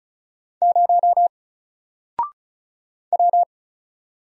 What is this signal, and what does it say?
Morse code, heard as a keyed tone, about 36 wpm, 700 Hz. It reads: 0 W
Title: Single Letter-Number Alt 35wpm